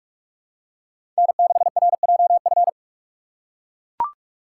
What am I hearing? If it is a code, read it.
Morse code, heard as a keyed tone, about 43 wpm, 700 Hz. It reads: N6RJF
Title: USA Call Signs - 5 Characters - 3x Character Spacing - 45wpm